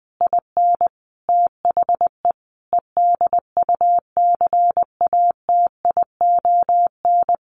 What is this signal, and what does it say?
Morse code, heard as a keyed tone, about 20 wpm, 700 Hz. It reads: IN THE EDUCATION
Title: Sentences from Top 1200 Words 20wpm